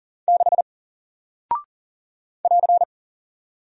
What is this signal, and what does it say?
Morse code, heard as a keyed tone, about 38 wpm, 700 Hz. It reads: 6 <AR>
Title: Single Character (Letter:Number:Punctuation:Prosign) Alt 40wpm